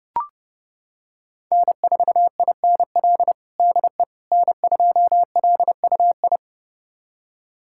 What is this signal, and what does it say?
Morse code, heard as a keyed tone, about 30 wpm, 700 Hz. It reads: N4INL DE N2LUI
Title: USA Callsign DE Callsign - 5 Characters - 30wpm